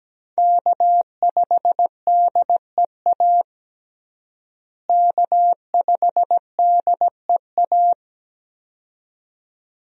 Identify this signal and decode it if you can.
Morse code, heard as a keyed tone, about 17 wpm, 700 Hz. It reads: K5DEA K5DEA
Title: USA Call Signs - 5 Characters - Repeated 2x - 17wpm